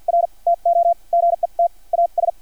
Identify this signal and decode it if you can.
Morse code, heard as a keyed tone, about 50 wpm, spaced as for 41 wpm, 690 Hz. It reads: R TO GET AS
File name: archived / CRSnet